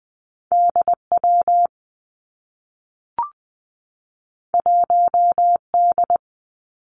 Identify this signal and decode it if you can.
Morse code, heard as a keyed tone, about 19 wpm, 700 Hz. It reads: D W 1 D
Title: Two Challenging Letter-Number Combinations 20wpm